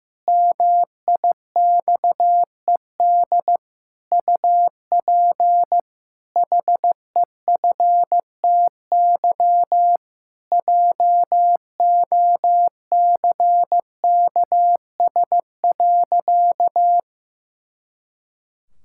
Morse code, heard as a keyed tone, about 16 wpm, 700 Hz. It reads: MIXED UP HEFTY JOCKS.
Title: Pangram Challenge v2 - 15wpm